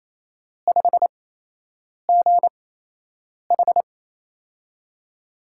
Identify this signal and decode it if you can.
Morse code, heard as a keyed tone, about 28 wpm, spaced as for 8 wpm, 700 Hz. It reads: 5GH
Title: USA Call Signs - 4 Characters - 8x Character Spacing - 28wpm